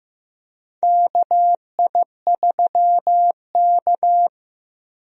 Morse code, heard as a keyed tone, about 15 wpm, 700 Hz. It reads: KI3K
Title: USA Call Signs - 4 Characters - Repeated 3x - 15wpm